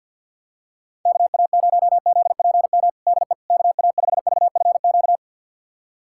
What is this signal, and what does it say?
Morse code, heard as a keyed tone, about 46 wpm, 700 Hz. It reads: KA0ZPM DE KA5VFQ
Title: USA Callsign DE Callsign - 6 Characters - 50wpm